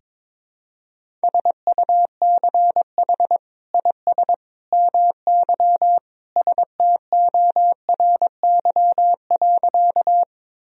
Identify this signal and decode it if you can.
Morse code, heard as a keyed tone, about 21 wpm, 700 Hz. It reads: SUCH IS MY STORY.
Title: Sentences from Top 1000 Words - Review Part 1 - 22wpm